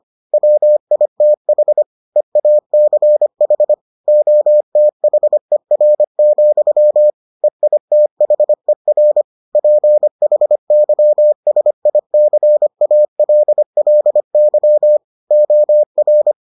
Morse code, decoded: WITH EACH OTHER, EITHER PHYSICALLY OR

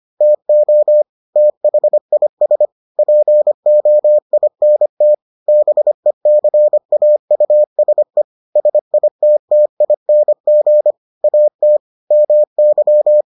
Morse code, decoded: TO THIS POINT BECAUSE SITTING AT MY

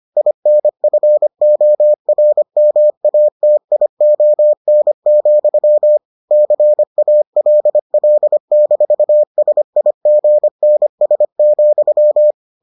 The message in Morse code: INFORMATION, CALL-SIGNS,

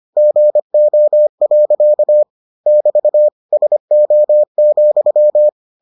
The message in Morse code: GO. <BT> SO,

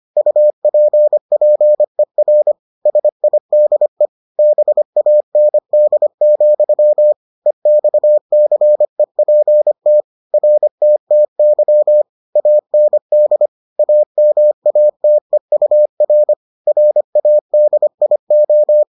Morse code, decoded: UPPER SIDE BAND, EXCEPT RTTY AND AMATEUR RADIO

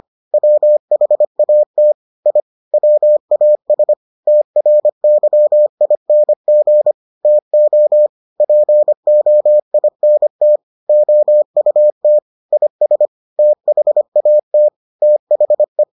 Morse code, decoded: WHAT I WAS TRYING TO POINT OUT IS THAT THE

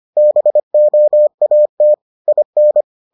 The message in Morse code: BOAT IN

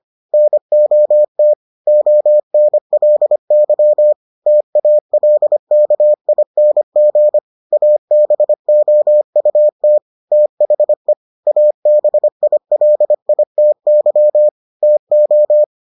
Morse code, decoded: NOT ONLY TALKING ABOUT THE ABILITY TO

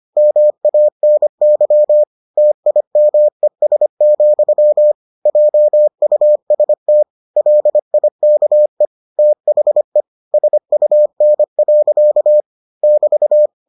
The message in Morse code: MANY TIMES, JUST LIKE THE SUN. <BT>

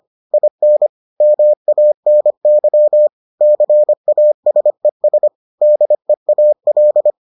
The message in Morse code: IN MANY CASES DEAL

